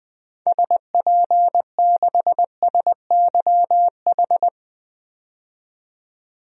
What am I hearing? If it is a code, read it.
Morse code, heard as a keyed tone, about 21 wpm, 700 Hz. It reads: SP6SYH
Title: DXCC-100 Call Signs - 6 Characters - 20wpm